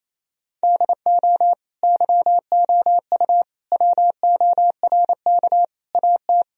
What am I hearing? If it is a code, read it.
Morse code, heard as a keyed tone, about 28 wpm, 700 Hz. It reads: DO YOU WORK AT
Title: Sentences from Top 1000 Words - Review Part 1 - 28wpm